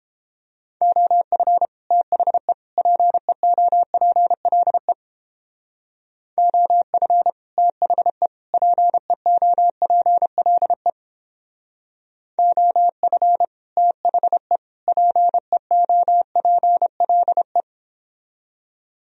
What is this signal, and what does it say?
Morse code, heard as a keyed tone, about 29 wpm, 700 Hz. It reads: OF THE PEOPLE OF THE PEOPLE OF THE PEOPLE